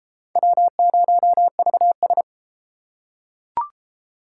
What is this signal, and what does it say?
Morse code, heard as a keyed tone, about 29 wpm, 700 Hz. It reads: W0VS